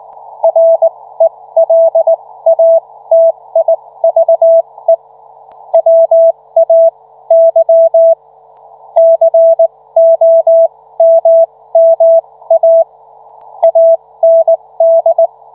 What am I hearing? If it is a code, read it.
Morse code, heard as a keyed tone, about 19 wpm, spaced as for 15 wpm, 660 Hz. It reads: RELATIVE WAY COMMA AND